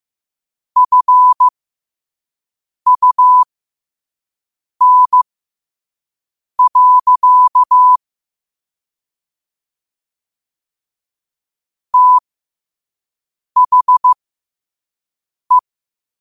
Morse code, heard as a keyed tone, about 15 wpm, spaced as for 5 wpm, 1000 Hz. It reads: FUN. THE